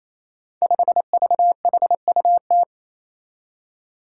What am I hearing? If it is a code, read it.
Morse code, heard as a keyed tone, about 27 wpm, 700 Hz. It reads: 5VHUT
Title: Five Challenging Letter-Number Combinations 28wpm